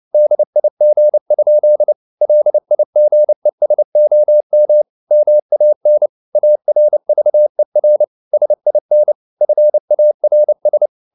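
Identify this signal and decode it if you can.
Morse code, heard as a keyed tone, about 29 wpm, 600 Hz. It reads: DIG? LIGESOM MAN ARVER SIN FARS
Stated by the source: Fast (QRQ) Morse at 29 wpm, from CW med Gnister.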